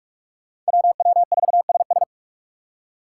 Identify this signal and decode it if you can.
Morse code, heard as a keyed tone, about 44 wpm, 700 Hz. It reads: WW4SS